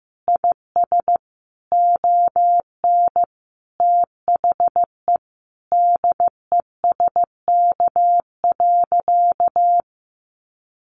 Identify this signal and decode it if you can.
Morse code, heard as a keyed tone, about 15 wpm, 700 Hz. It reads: IS ON THE DESK.
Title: Additional Easy Phrases 15wpm